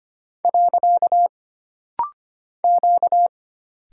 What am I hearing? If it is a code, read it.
Morse code, heard as a keyed tone, about 25 wpm, 700 Hz. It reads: . Q
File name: Single Character (Letter:Number:Punctuation:Prosign) - Rapid-Fire - 25wpm